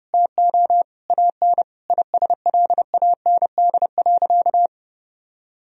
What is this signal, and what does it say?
Morse code, heard as a keyed tone, about 30 wpm, 700 Hz. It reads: TO AN ISLAND.
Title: Sentences from Top 1000 Words - Review Part 2 - 30wpm